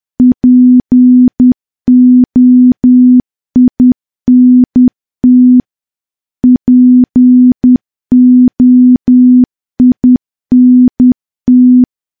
Morse code, heard as a keyed tone, about 10 wpm, 250 Hz. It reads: POINT POINT
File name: morse / morse-training-10WPM